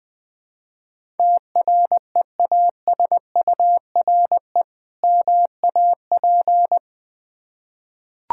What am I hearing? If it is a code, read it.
Morse code, heard as a keyed tone, about 20 wpm, 700 Hz. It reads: TREASURE MAP
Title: Difficult Words with Hints - 20wpm